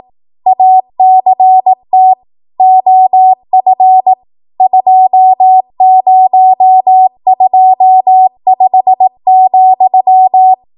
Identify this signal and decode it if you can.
Morse code, heard as a keyed tone, about 18 wpm, 750 Hz. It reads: ACT OF 2025,
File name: Morse / 18WPM